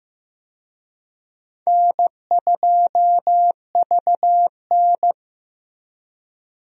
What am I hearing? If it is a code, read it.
Morse code, heard as a keyed tone, about 15 wpm, 700 Hz. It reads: N2VN
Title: USA Call Signs - 4 Characters - Repeated 2x - 15wpm